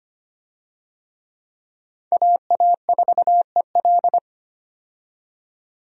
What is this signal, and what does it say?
Morse code, heard as a keyed tone, about 25 wpm, 700 Hz. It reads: AA4EL
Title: USA Call Signs - 5 Characters - 25wpm